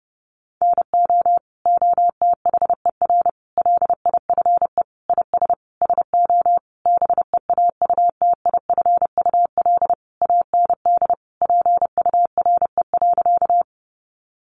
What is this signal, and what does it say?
Morse code, heard as a keyed tone, about 30 wpm, 700 Hz. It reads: NO OTHER LIFE IS SO BEAUTIFUL AND PURE.
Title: Sentences from Top 1500 Words 30wpm